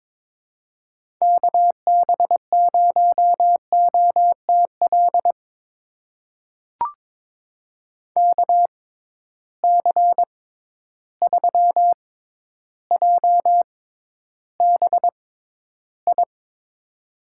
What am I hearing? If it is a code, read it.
Morse code, heard as a keyed tone, about 21 wpm, spaced as for 11 wpm, 700 Hz. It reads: KB0OTL KC3JBI
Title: USA Call Signs - 6 Characters - 6x Character Spacing - 22wpm